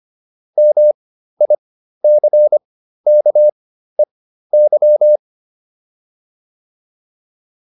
Morse code, heard as a keyed tone, about 25 wpm, spaced as for 13 wpm, 600 Hz. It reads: MICKEY
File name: QSO310 13